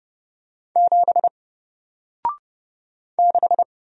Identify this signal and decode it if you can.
Morse code, heard as a keyed tone, about 30 wpm, 700 Hz. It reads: 7 6
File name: Single-Letter-Number-30wpm